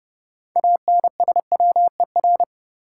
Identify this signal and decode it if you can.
Morse code, heard as a keyed tone, about 30 wpm, 700 Hz. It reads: ANSWER
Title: Sentences from Top 300 Words - New Word Drill - 30wpm